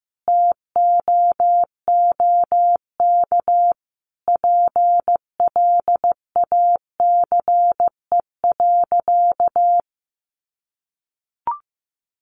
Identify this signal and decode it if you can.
Morse code, heard as a keyed tone, about 14 wpm, 700 Hz. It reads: TOOK PLACE.
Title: Sentences from Top 1500 Words 15wpm